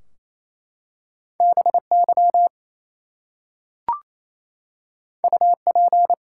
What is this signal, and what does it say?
Morse code, heard as a keyed tone, about 29 wpm, 700 Hz. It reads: BY UP
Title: Two Letter Words - 28wpm